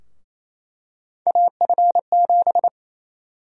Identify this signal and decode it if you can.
Morse code, heard as a keyed tone, about 28 wpm, 700 Hz. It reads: AF7
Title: USA Call Sign Prefixes - 28wpm